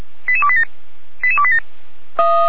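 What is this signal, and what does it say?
Five-tone selective call (ZVEI), heard as quick runs of five tones, tones of about 70 ms. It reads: 88287 88287
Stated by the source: ZVEI five-tone call 88287 received by SDR, archived on the Feuerwehr Orlamünde fire brigade server.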